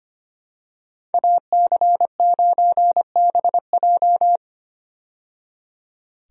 Morse code, heard as a keyed tone, about 25 wpm, 700 Hz. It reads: AC9BJ
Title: USA Call Signs - 5 Characters - 25wpm